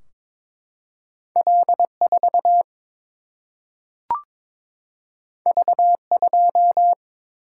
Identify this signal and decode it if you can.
Morse code, heard as a keyed tone, about 22 wpm, 700 Hz. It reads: L 4 V 2
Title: Single-Letter Single-Number 22wpm